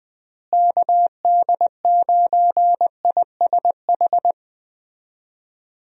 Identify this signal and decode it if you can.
Morse code, heard as a keyed tone, about 20 wpm, 700 Hz. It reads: KD9ISH
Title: USA Call Signs - 6 Characters - Repeated 3x - 20wpm